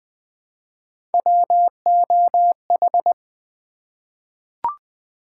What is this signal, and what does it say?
Morse code, heard as a keyed tone, about 20 wpm, 700 Hz. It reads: WOH
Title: Three Letter-Number Combinations - 20wpm